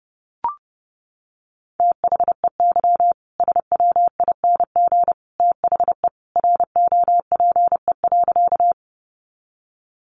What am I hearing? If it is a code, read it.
Morse code, heard as a keyed tone, about 30 wpm, 700 Hz. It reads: THEY SWING THE ROPE.
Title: Easy Phrases 30wpm